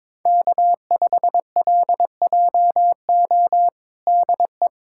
Morse code, heard as a keyed tone, about 23 wpm, 700 Hz. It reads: K5LJO DE
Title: USA Callsign DE Callsign - 5 Characters - 22wpm